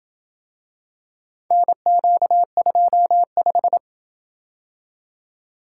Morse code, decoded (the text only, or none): NQ25